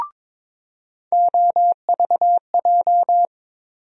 Morse code, heard as a keyed tone, about 22 wpm, 700 Hz. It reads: OVJ
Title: USA Call Sign Postfixes - 3 Letters 22wpm